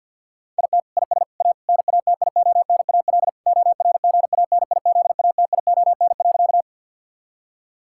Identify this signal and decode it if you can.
Morse code, heard as a keyed tone, about 50 wpm, 700 Hz. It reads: IT IS A NATIONAL ORGANIZATION.
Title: Sentences from Top 1000 Words - Review Part 5 - 50wpm